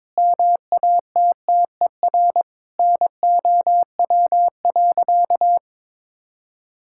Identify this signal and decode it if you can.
Morse code, heard as a keyed tone, about 21 wpm, 700 Hz. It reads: MATTER NOW.
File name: practice / 22wpm / Sentences with 4-Word Phrases - 22wpm